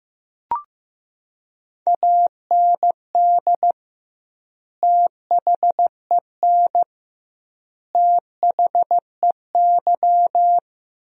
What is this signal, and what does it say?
Morse code, heard as a keyed tone, about 16 wpm, 700 Hz. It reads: AND THEN THEY
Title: 2x Word-Spacing - Sentences from Top 300 Words - 15wpm